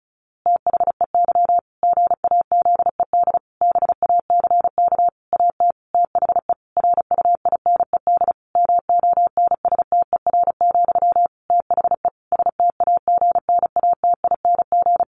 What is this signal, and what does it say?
Morse code, heard as a keyed tone, about 34 wpm, 700 Hz. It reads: THEY GAZED BACK AT THE RUINED MONSTER, THE STAGNATING
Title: A Sound Of Thunder 35wpm